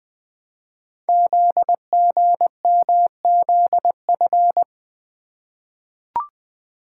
Morse code, heard as a keyed tone, about 20 wpm, 700 Hz. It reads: ZGMZF
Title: Code Groups - 5 Letters - 20wpm